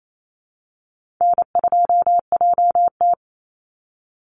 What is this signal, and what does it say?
Morse code, heard as a keyed tone, about 28 wpm, 700 Hz. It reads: N2JT
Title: Easy Call Signs Repeated 3x - 28wpm